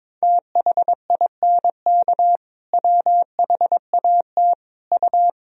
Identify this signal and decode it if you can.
Morse code, heard as a keyed tone, about 22 wpm, 700 Hz. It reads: THINK WHAT U
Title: Sentences from Top 200 Words and Abbreviations 22wpm